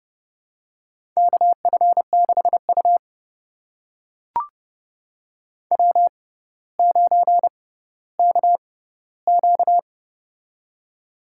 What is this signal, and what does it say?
Morse code, heard as a keyed tone, about 30 wpm, spaced as for 15 wpm, 700 Hz. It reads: KF6U W9KQ
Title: USA Call Signs - 4 Characters - 6x Character Spacing - 30wpm